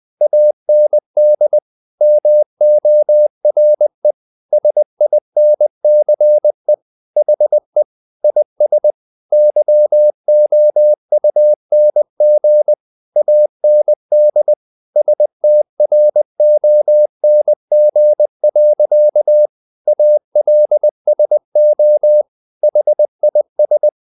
Morse code, heard as a keyed tone, about 20 wpm, 600 Hz. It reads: AND MORE SINCE HE IS YOUNG AND STRONG. ALSO HIS
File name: the old man and the sea / the old man and the sea part 1 20wpm